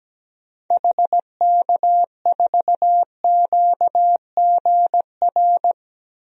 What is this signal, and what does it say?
Morse code, heard as a keyed tone, about 17 wpm, 700 Hz. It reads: HK4QGR
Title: DXCC-100 Call Signs - 6 Characters - 17wpm